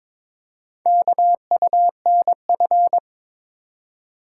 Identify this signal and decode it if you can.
Morse code, heard as a keyed tone, about 22 wpm, 690 Hz. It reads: KUNF
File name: practice / 22wpm / Four Letter-Number Combinations - 22wpm